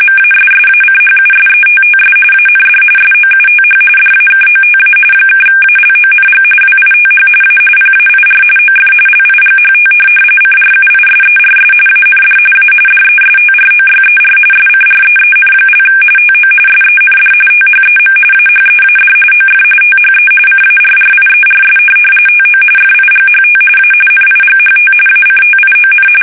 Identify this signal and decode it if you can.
RTTY, about 45.45 baud. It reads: MASSACHUSETTS AND SOUTH DAKOTA (ALONG WITH A PAIR OF LIGHT CRUISERS AND A DESTROYER SCREEN) TO SEND THEM AHEAD OF THE CARRIERS TO FIGHT A NIGHT ACTION
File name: fsk850 / USS-Massachusetts